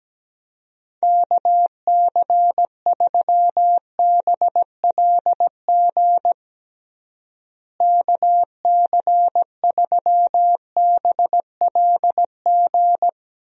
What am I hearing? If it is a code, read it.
Morse code, heard as a keyed tone, about 17 wpm, 700 Hz. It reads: KC3BLG KC3BLG
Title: USA Call Signs - 6 Characters - Repeated 3x - 17wpm